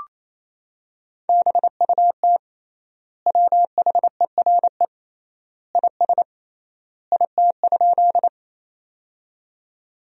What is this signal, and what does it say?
Morse code, heard as a keyed tone, about 28 wpm, 700 Hz. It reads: BUT WHERE IS IT?
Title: 3x Word-Spacing - Sentences from Top 200 Words - 28wpm